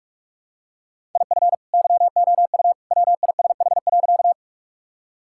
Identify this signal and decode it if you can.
Morse code, heard as a keyed tone, about 46 wpm, 700 Hz. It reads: IF YOU WISH.
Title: Sentences from Top 1600 Words 45wpm